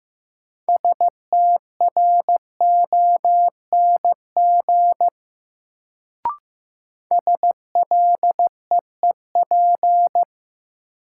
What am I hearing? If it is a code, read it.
Morse code, heard as a keyed tone, about 15 wpm, 700 Hz. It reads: STRONG SLEEP
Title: Sentences from Top 500 Words - New Word Drill - 15wpm